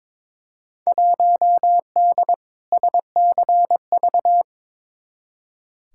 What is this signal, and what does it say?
Morse code, heard as a keyed tone, about 21 wpm, 700 Hz. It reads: 1D SCV
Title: Field Day Exchanges Encore - 22wpm